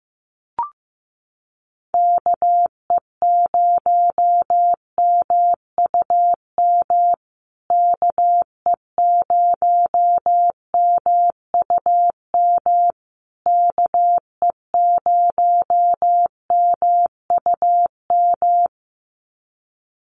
Morse code, heard as a keyed tone, about 15 wpm, 700 Hz. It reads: KE0MUM KE0MUM KE0MUM
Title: USA Call Signs Repeated 3x 15wpm